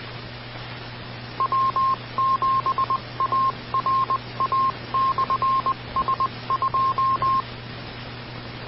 Morse code, heard as a keyed tone, about 20 wpm, 1060 Hz. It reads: W7ARA/S2